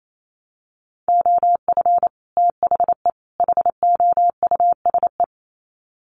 Morse code, heard as a keyed tone, about 28 wpm, 700 Hz. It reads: OF THE HOUSE
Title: Sets of 3 Words 28wpm